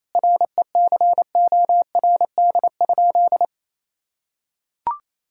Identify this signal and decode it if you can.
Morse code, heard as a keyed tone, about 28 wpm, 700 Hz. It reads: RECORD?
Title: Sentences from Top 1000 Words - Review Part 6 - 28wpm